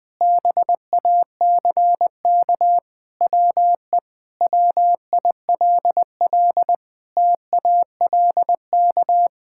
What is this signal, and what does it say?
Morse code, heard as a keyed tone, about 21 wpm, 700 Hz. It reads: BACK WE WILL TALK
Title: Sentences with 5-Word Phrases - 20wpm